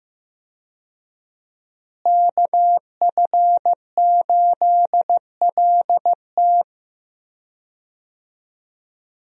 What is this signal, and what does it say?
Morse code, heard as a keyed tone, about 15 wpm, 700 Hz. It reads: KF8LT
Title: USA Call Signs - 5 Characters - Repeated 3x - 15wpm